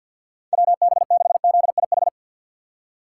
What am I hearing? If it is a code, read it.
Morse code, heard as a keyed tone, about 52 wpm, 700 Hz. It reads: WB6ZIH
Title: USA Call Signs - 6 Characters - 4x Character Spacing - 50wpm